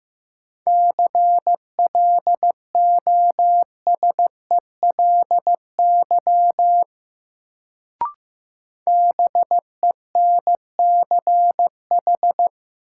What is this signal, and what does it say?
Morse code, heard as a keyed tone, about 15 wpm, 700 Hz. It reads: CLOSELY BENCH
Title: Sentences from Top 1400 Words - New Word Drill - 15wpm